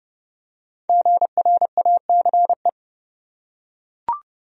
Morse code, heard as a keyed tone, about 30 wpm, 700 Hz. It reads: GRACE
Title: USA Common Names - Max 5 Letters - 30wpm